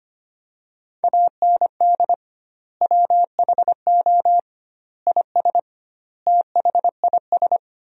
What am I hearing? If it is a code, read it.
Morse code, heard as a keyed tone, about 25 wpm, 700 Hz. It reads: AND WHO IS THIS
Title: 2x Word-Spacing - Sentences from Top 300 Words - 25wpm